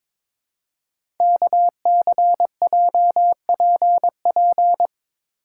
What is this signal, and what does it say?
Morse code, heard as a keyed tone, about 23 wpm, 700 Hz. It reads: KCJPP
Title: Code Groups - 5 Letters - 22wpm